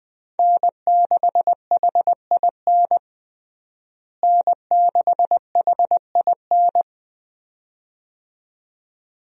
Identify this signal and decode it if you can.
Morse code, heard as a keyed tone, about 21 wpm, 700 Hz. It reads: N6HIN N6HIN
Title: USA Call Signs - 5 Characters - Repeated 2x - 20wpm